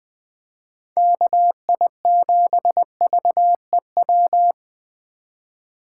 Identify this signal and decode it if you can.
Morse code, heard as a keyed tone, about 21 wpm, 700 Hz. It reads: KI7VEW
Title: USA Call Signs - 6 Characters - 4x Character Spacing - 20wpm